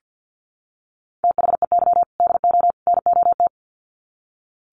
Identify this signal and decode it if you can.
Morse code, heard as a keyed tone, about 48 wpm, 700 Hz. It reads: THEY DO NOT
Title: Sets of 3 Words 50wpm